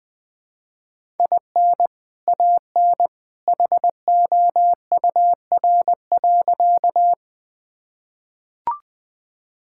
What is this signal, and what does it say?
Morse code, heard as a keyed tone, about 21 wpm, 700 Hz. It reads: IN AN HOUR.
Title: Sentences with 2-Word Phrases - Spoken Beginning - 20wpm